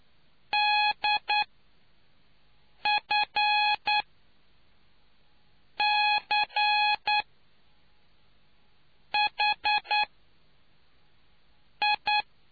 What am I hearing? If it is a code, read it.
Morse code, heard as a keyed tone, about 9 wpm, spaced as for 4 wpm, 840 Hz. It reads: DFCHI